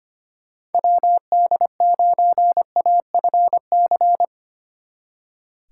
Morse code, heard as a keyed tone, about 32 wpm, 700 Hz. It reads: WD9AFC